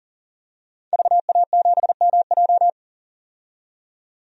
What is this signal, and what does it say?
Morse code, heard as a keyed tone, about 39 wpm, 700 Hz. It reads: VA7MJ
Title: DXCC-100 Call Signs - 5 Characters - 40wpm